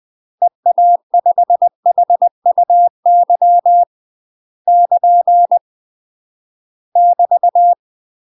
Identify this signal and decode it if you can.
Morse code, heard as a keyed tone, about 20 wpm, 700 Hz. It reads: EA5HUY <KN> <BT>
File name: UK MASTER QSOs TRAINING / 20WPM / UK MASTER QSOs 0066